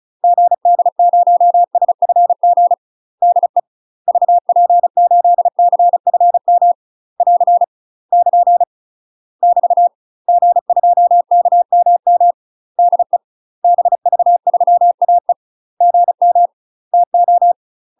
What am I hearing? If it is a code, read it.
Morse code, heard as a keyed tone, about 34 wpm, 700 Hz. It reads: GD0SFG DE VP8CFM <AR> <KN> <BT> G2KMM DE BV3AE GM TO
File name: UK MASTER QSOs TRAINING/35WPM